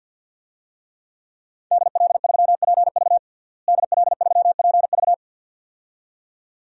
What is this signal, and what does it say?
Morse code, heard as a keyed tone, about 47 wpm, 700 Hz. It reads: DL3PV DL3PV
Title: DXCC-100 Call Signs - 5 Characters - Repeated 2x - 50wpm